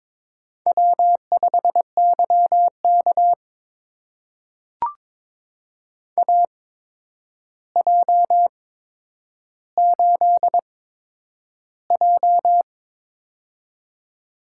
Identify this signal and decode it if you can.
Morse code, heard as a keyed tone, about 21 wpm, spaced as for 9 wpm, 700 Hz. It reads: W5YK AJ8J
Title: USA Call Signs - 4 Characters - 8x Character Spacing - 22wpm